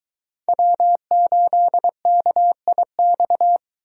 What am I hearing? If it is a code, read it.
Morse code, heard as a keyed tone, about 22 wpm, 700 Hz. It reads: W8KIX